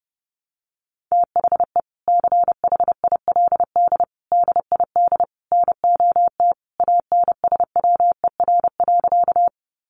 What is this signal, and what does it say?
Morse code, heard as a keyed tone, about 30 wpm, 700 Hz. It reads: THE CHILD DID NOT ANSWER.
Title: Sentences from Top 800 Words 30wpm